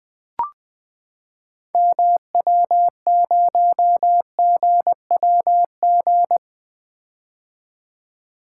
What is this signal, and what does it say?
Morse code, heard as a keyed tone, about 20 wpm, 700 Hz. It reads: MW0GWG
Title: 2023 POTA Activator Call Signs - 20wpm